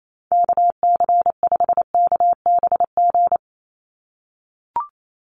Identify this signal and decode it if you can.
Morse code, heard as a keyed tone, about 28 wpm, 700 Hz. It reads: KC5KBG
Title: USA Call Signs 28wpm